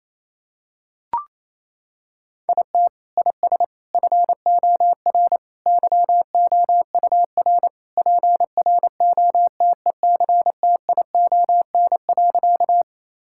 Morse code, heard as a keyed tone, about 28 wpm, 700 Hz. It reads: IT IS FOR YOUR PROTECTION.
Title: Sentences from Top 1600 Words 28wpm